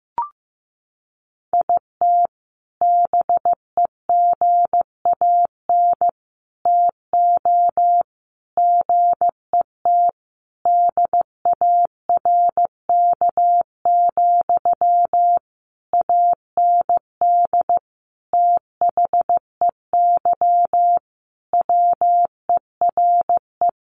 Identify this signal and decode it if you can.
Morse code, heard as a keyed tone, about 15 wpm, 700 Hz. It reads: IT BEGAN TO GET DARK, AND THEY WERE
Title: Sentences from Top 1400 Words 15wpm